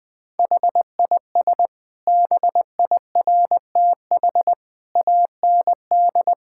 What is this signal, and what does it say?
Morse code, heard as a keyed tone, about 20 wpm, 700 Hz. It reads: HIS BIRTH AND
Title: Sentences from Top 1000 Words - Review Part 5 - 20wpm